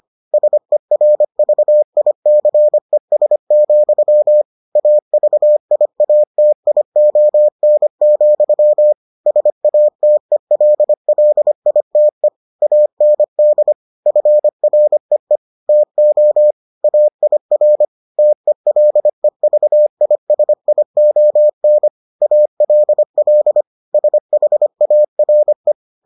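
Morse code, heard as a keyed tone, about 25 wpm, 600 Hz. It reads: SERVICES, AVIATION, SATELLITE AND FREE TO AIR TELEVISION ALL SHARE
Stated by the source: Morse edition of the Foundations of Amateur Radio episodes.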